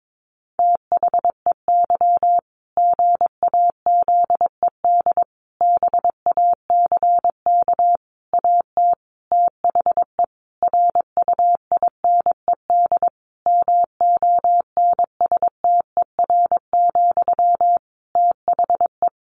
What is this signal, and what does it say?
Morse code, heard as a keyed tone, about 22 wpm, 700 Hz. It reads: THEY GAZED BACK AT THE RUINED MONSTER, THE
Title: A Sound Of Thunder 22wpm